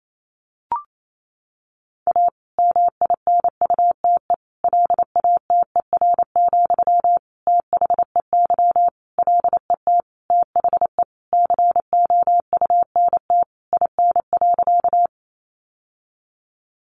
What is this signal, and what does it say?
Morse code, heard as a keyed tone, about 28 wpm, 700 Hz. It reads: A MINUTE LATER, THEY LET THE COUNT IN.
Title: Sentences from Top 1500 Words 28wpm